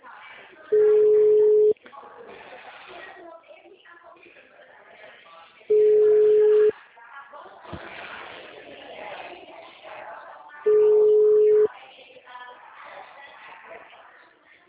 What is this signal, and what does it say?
Ringback tone (Europe, 425 Hz), with speech.
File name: out- -2011- - - .40